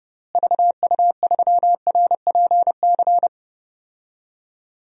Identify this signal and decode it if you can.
Morse code, heard as a keyed tone, about 30 wpm, 700 Hz. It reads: VU3RPC
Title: DXCC-100 Call Signs - 6 Characters - Repeated 2x - 30wpm